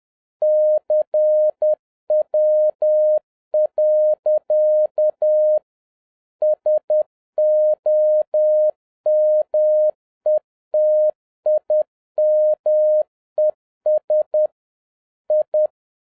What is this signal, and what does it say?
Morse code, heard as a keyed tone, about 10 wpm, 610 Hz. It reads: CW. SOMETIMES I